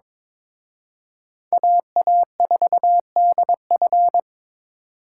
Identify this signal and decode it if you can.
Morse code, heard as a keyed tone, about 26 wpm, 700 Hz. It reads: AA4DF
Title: Speed Racing - USA 2x2 Call Signs 22wpm